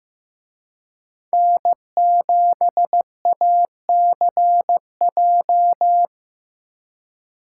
Morse code, heard as a keyed tone, about 17 wpm, 700 Hz. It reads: N7ACJ
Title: Speed Racing - USA Call Signs 15wpm